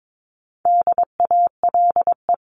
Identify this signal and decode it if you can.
Morse code, heard as a keyed tone, about 22 wpm, 700 Hz. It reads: DALE
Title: CWT Exchange 22wpm